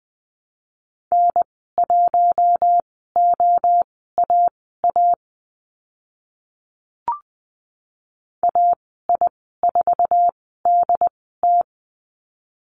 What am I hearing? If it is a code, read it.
Morse code, heard as a keyed tone, about 20 wpm, spaced as for 14 wpm, 700 Hz. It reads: N1OAA AI4DT
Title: 069 Lesson 12g - Letter d - Call Signs 20wpm